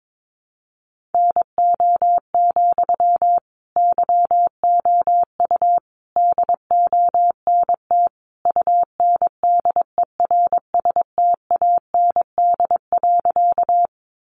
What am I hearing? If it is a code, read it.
Morse code, heard as a keyed tone, about 22 wpm, 700 Hz. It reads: NO, YOU DONT UNDERSTAND.
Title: Sentences from Top 600 Words 22wpm